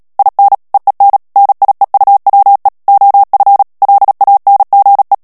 Morse code, heard as a keyed tone, about 37 wpm, 800 Hz. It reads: IN EEN NIEUWE OF LANGE
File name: morse mp3 / HOLLANDAIS texte clair 2 vitesse 35 mots minute